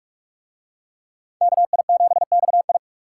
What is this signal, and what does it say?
Morse code, heard as a keyed tone, about 42 wpm, 700 Hz. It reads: KI7XI